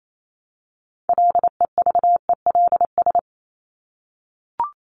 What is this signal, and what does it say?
Morse code, heard as a keyed tone, about 28 wpm, 700 Hz. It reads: LEVELS
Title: ES:S Suffix - Top 100 Words with Base Words 28wpm